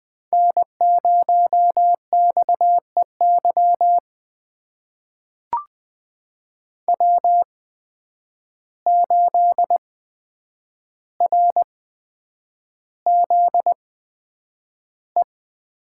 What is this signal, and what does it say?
Morse code, heard as a keyed tone, about 20 wpm, spaced as for 9 wpm, 700 Hz. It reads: N0XEY W8RZE
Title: USA Call Signs - 5 Characters - 8x Character Spacing - 20wpm